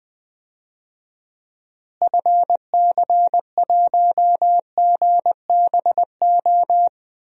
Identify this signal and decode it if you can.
Morse code, heard as a keyed tone, about 20 wpm, 700 Hz. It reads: FC1GBO
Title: DXCC-100 Call Signs - 6 Characters - Repeated 2x - 20wpm